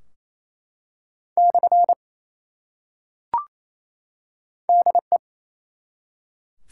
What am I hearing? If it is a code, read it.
Morse code, heard as a keyed tone, about 28 wpm, 700 Hz. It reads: / DE
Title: Top 180 QSO Elements 28wpm